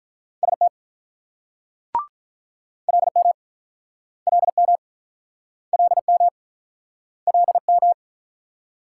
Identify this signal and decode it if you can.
Morse code, heard as a keyed tone, about 49 wpm, 700 Hz. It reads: ST LM LM LM LM